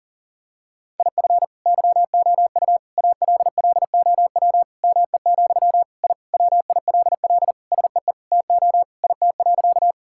Morse code, heard as a keyed tone, about 37 wpm, 700 Hz. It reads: IF YOU ALLOW ME, I WILL SEE TO IT.
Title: Sentences from Top 1000 Words - Review Part 5 - 40wpm